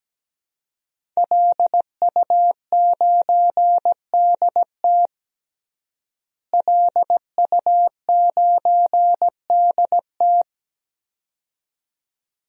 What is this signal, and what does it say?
Morse code, heard as a keyed tone, about 17 wpm, 700 Hz. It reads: LU9DT LU9DT
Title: DXCC-100 Call Signs - 5 Characters - Repeated 2x - 17wpm